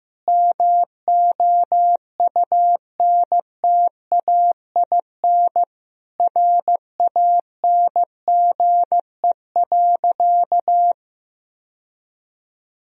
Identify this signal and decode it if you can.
Morse code, heard as a keyed tone, about 15 wpm, 700 Hz. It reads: MOUNTAIN RANGE.
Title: Sentences from Top 1000 Words - Review Part 4 - 15wpm